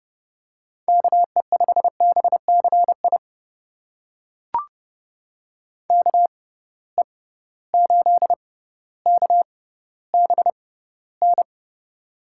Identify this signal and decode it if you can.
Morse code, heard as a keyed tone, about 30 wpm, spaced as for 15 wpm, 700 Hz. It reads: KE5BCI KE8KBN
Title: USA Call Signs - 6 Characters - 6x Character Spacing - 30wpm